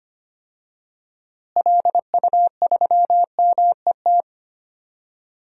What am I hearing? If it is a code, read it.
Morse code, heard as a keyed tone, about 25 wpm, 700 Hz. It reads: LU3MET
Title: DXCC-100 Call Signs - 6 Characters - Repeated 3x - 25wpm